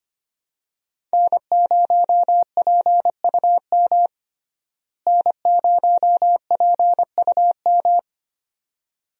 Morse code, heard as a keyed tone, about 25 wpm, 700 Hz. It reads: N0PUM N0PUM
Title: USA Call Signs - 5 Characters - Repeated 2x - 25wpm